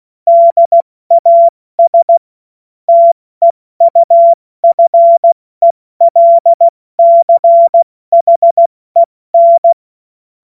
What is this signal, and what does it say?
Morse code, heard as a keyed tone, about 16 wpm, 680 Hz. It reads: DAS TEUFELCHEN
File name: Morsekurs / Witze / Witz 28.November 2023 70 F80